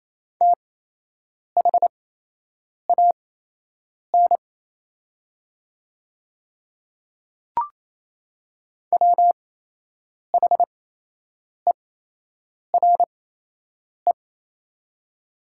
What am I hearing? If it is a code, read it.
Morse code, heard as a keyed tone, about 28 wpm, spaced as for 8 wpm, 700 Hz. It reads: THAN WHERE
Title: 8x Character Spacing - Top 200 Words - 28wpm